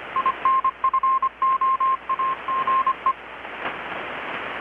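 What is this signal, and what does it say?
Morse code, heard as a keyed tone, about 25 wpm, 1060 Hz, with radio noise. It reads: INFOAGE
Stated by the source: Morse from KPH's Night of Nights XX maritime radiotelegraphy event, received in Chiapas, Mexico about 3,800 km away on a Kenwood R-600 with a grounded 10 m long-wire antenna.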